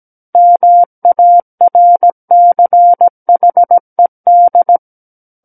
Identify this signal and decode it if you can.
Morse code, heard as a keyed tone, about 17 wpm, 700 Hz. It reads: MARCHED
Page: SKCC Learning Center - Morse Code Practice Files